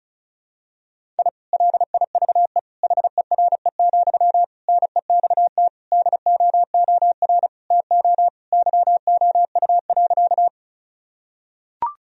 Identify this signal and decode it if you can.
Morse code, heard as a keyed tone, about 36 wpm, 700 Hz. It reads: I LIVE HERE, NEXT DOOR TO YOU.
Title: Sentences from Top 1000 Words - Review Part 1 - 35wpm